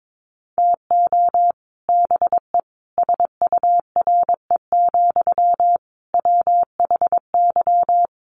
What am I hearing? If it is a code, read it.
Morse code, heard as a keyed tone, about 22 wpm, 700 Hz. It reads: TO BE SURE, WHY
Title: Sentences from Top 600 Words 22wpm